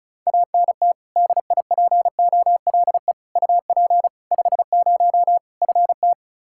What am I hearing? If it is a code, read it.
Morse code, heard as a keyed tone, about 34 wpm, 700 Hz. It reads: ANT DIPOLE UP 50 FT
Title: Practice QSO - Exchange 2 - Info - 35wpm